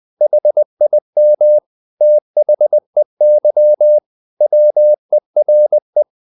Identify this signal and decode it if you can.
Morse code, heard as a keyed tone, about 20 wpm, 600 Hz. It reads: HIM THEY WERE
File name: the old man and the sea / the old man and the sea part 1 20wpm